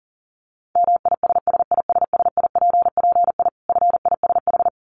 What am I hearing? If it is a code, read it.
Morse code, heard as a keyed tone, about 40 wpm, 700 Hz. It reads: MISSISSIPPI FISH
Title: Dit Challenge 40wpm